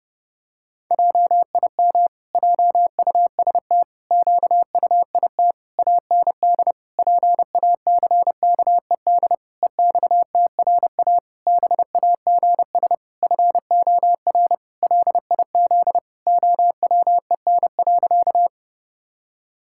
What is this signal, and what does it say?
Morse code, heard as a keyed tone, about 30 wpm, 700 Hz. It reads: JIM JUST QUIT AND PACKED EXTRA BAGS FOR LIZ OWEN.
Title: Pangram Challenge v2 - 30wpm